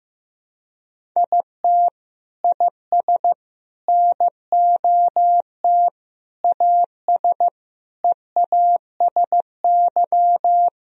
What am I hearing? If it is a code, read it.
Morse code, heard as a keyed tone, about 15 wpm, 700 Hz. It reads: IT IS NOT AS EASY
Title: Sentences from Top 1000 Words - Review Part 1 - 15wpm